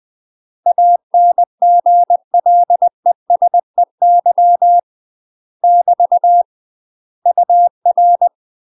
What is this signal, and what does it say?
Morse code, heard as a keyed tone, about 20 wpm, 700 Hz. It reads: ANGLESEY <BT> UR
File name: UK MASTER QSOs TRAINING/20WPM